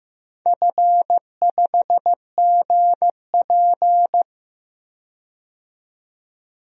Morse code, heard as a keyed tone, about 15 wpm, 700 Hz. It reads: F5GP
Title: DXCC-100 Call Signs - 4 Characters - 15wpm